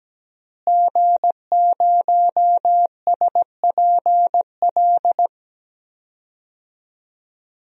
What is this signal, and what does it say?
Morse code, heard as a keyed tone, about 17 wpm, 700 Hz. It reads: G0SPL
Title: DXCC-100 Call Signs - 5 Characters - Repeated 3x - 17wpm